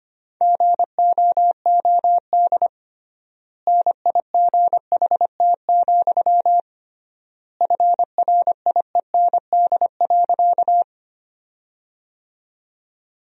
Morse code, heard as a keyed tone, about 25 wpm, 700 Hz. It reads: GOOD NIGHT, FRIEND.
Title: 3x Word-Spacing - Sentences from Top 300 Words - 25wpm